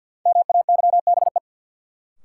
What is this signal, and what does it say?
Morse code, heard as a keyed tone, about 52 wpm, 700 Hz. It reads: MAYBE
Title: Sentences from Top 1000 Words - New Word Drill - 50wpm